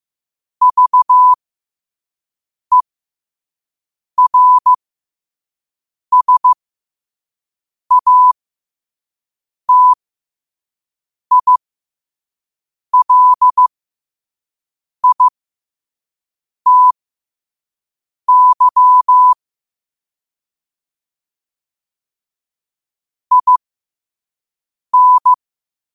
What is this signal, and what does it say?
Morse code, heard as a keyed tone, about 15 wpm, spaced as for 5 wpm, 1000 Hz. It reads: VERSATILITY IN
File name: qotdfiles / 5wpm